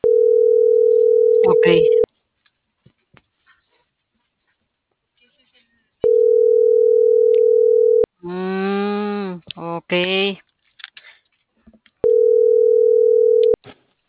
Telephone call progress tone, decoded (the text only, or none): ringback tone